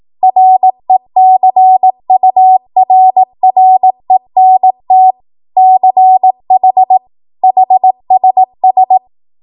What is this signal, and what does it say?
Morse code, heard as a keyed tone, about 18 wpm, 750 Hz. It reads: RECURRENT CH HSS
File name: Morse / 18WPM